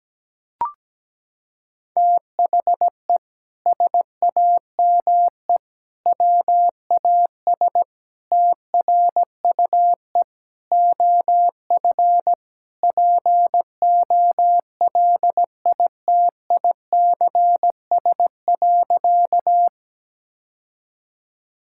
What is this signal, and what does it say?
Morse code, heard as a keyed tone, about 17 wpm, 700 Hz. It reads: THE SAME WAS TRUE OF POLITICS.
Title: Sentences from Top 1000 Words - Review Part 6 - 17wpm